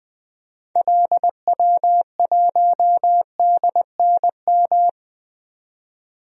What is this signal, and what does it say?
Morse code, heard as a keyed tone, about 20 wpm, 700 Hz. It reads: LW1DNM
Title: DXCC-100 Call Signs - 6 Characters - 20wpm